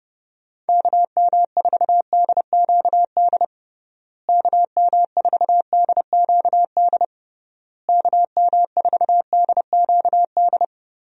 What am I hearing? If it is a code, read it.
Morse code, heard as a keyed tone, about 30 wpm, 700 Hz. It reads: KM4DQD KM4DQD KM4DQD
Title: USA Call Signs - 6 Characters - Repeated 3x - 30wpm